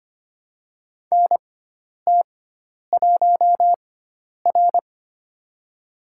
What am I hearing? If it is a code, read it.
Morse code, heard as a keyed tone, about 26 wpm, spaced as for 10 wpm, 700 Hz. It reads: NT1R
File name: Lesson 16g - Letter c - Call Signs 25wpm